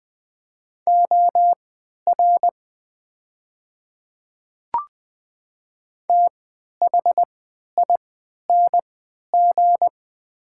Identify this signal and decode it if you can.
Morse code, heard as a keyed tone, about 21 wpm, spaced as for 12 wpm, 700 Hz. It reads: OR THING
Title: 3x Character Spacing - Top 100 Words - 20wpm